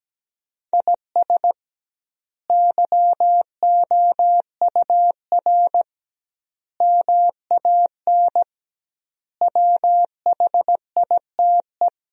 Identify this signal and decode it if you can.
Morse code, heard as a keyed tone, about 17 wpm, 700 Hz. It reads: IS YOUR MAN WHITE
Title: 2x Word-Spacing - Sentences from Top 400 Words - 17wpm